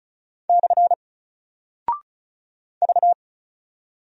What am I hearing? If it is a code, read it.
Morse code, heard as a keyed tone, about 36 wpm, 700 Hz. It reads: / V
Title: Single Character (Letter:Number:Punctuation:Prosign) Alt 35wpm